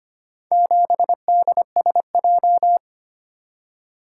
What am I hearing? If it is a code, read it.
Morse code, heard as a keyed tone, about 25 wpm, 700 Hz. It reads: 7DSJ
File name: practice / 25wpm / Four Letter-Number Combinations - 25wpm